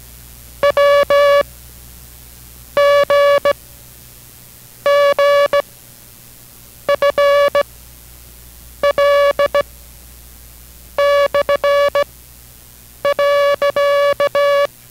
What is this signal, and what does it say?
Morse code, heard as a keyed tone, about 17 wpm, 570 Hz. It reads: W G G F L / .